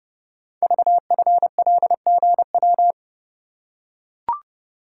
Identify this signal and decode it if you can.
Morse code, heard as a keyed tone, about 29 wpm, 700 Hz. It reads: VFLGW
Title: Code Groups - 5 Letters - 30wpm